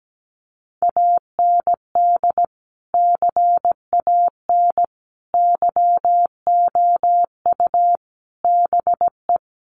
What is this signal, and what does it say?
Morse code, heard as a keyed tone, about 17 wpm, 700 Hz. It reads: AND CAN YOU BE
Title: Sentences from Top 1100 Words 17wpm